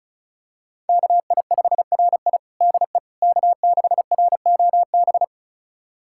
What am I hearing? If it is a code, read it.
Morse code, heard as a keyed tone, about 34 wpm, 690 Hz. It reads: KI5RI DE K6ROB
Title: USA Callsign DE Callsign - 5 Characters - 35wpm